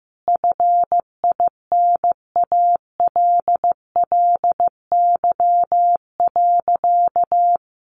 Morse code, heard as a keyed tone, about 14 wpm, 700 Hz. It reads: FINALLY.
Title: Sentences from Top 1000 Words 15wpm